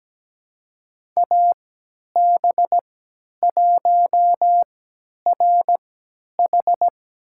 Morse code, heard as a keyed tone, about 17 wpm, spaced as for 10 wpm, 700 Hz. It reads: AB1RH
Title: USA Call Signs - 5 Characters - 3x Character Spacing - 17wpm